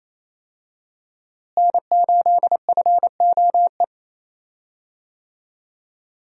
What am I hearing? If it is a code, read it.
Morse code, heard as a keyed tone, about 28 wpm, 700 Hz. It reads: N8FOE
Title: USA Call Signs - 5 Characters - Repeated 2x - 28wpm